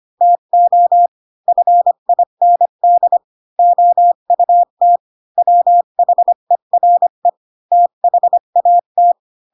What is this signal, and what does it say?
Morse code, heard as a keyed tone, about 25 wpm, 700 Hz. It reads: TO FIND OUT WHERE THAT
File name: Pinocchio / Pinocchio ch1 20 25 ik1hge